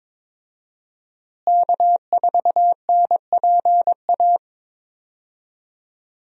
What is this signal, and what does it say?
Morse code, heard as a keyed tone, about 22 wpm, 700 Hz. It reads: K4NPA
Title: USA Call Signs - 5 Characters - Repeated 3x - 22wpm